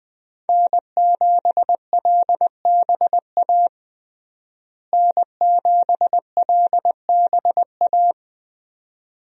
Morse code, heard as a keyed tone, about 20 wpm, 700 Hz. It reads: N7LBA N7LBA
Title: USA Call Signs - 5 Characters - Repeated 2x - 20wpm